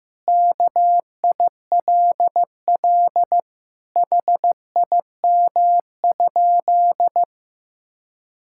Morse code, heard as a keyed tone, about 16 wpm, 700 Hz. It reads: KILL HIM?
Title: Sentences from Top 1600 Words 15wpm